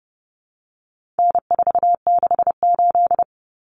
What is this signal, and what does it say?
Morse code, heard as a keyed tone, about 30 wpm, 700 Hz. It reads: N468
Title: Common Cut Numbers - 4-Digit 30wpm